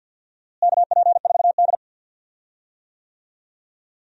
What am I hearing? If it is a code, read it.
Morse code, heard as a keyed tone, about 47 wpm, 700 Hz. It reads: KP4D
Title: DXCC-100 Call Signs - 4 Characters - Repeated 2x - 50wpm